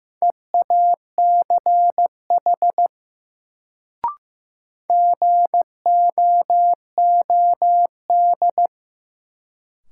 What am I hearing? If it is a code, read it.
Morse code, heard as a keyed tone, about 16 wpm, 700 Hz. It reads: EACH GOOD
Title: Sentences from Top 200 Words - New Word Drill - 15wpm